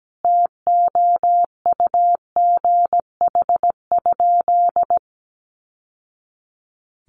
Morse code, heard as a keyed tone, about 17 wpm, 700 Hz. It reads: TOUGH?
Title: Sentences from Top 1200 Words 17wpm